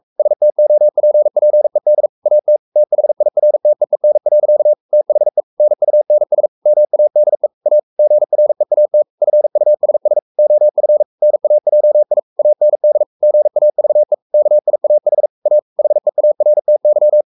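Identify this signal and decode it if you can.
Morse code, heard as a keyed tone, about 44 wpm, 600 Hz. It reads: STOPPED AT THIRTEEN. THE NUNS MADE A GREAT FUSS OF NAJI AND GAVE KIAH A HEARTY
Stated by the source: Morse code at 43 wpm sending Ken Follett's novel Never in English, from CW med Gnister.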